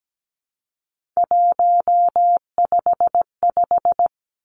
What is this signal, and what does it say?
Morse code, heard as a keyed tone, about 17 wpm, 700 Hz. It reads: 155
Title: Three Challenging Letter-Number Combinations 17wpm